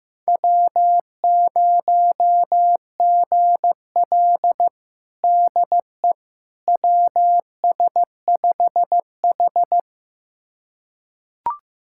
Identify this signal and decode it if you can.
Morse code, heard as a keyed tone, about 15 wpm, 700 Hz. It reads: W0GL DE WS5H
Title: USA Callsign DE Callsign - 4 Characters - 15wpm